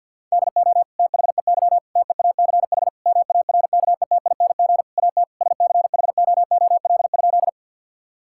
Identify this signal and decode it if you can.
Morse code, heard as a keyed tone, about 48 wpm, 690 Hz. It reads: DO THEY TEACH MARKETING AT SCHOOL?
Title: Sentences from Top 1000 Words - Review Part 5 - 50wpm